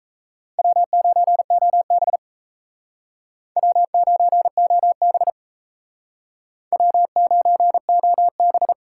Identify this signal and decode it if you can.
Morse code, heard as a keyed tone, about 37 wpm, 700 Hz. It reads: W9OB W9OB W9OB